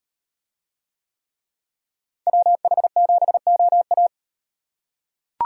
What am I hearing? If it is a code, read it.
Morse code, heard as a keyed tone, about 38 wpm, 700 Hz. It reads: WH7OA